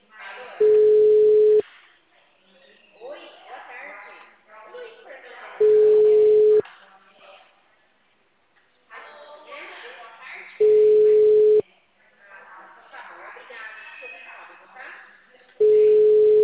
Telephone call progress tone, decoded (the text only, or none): ringback tone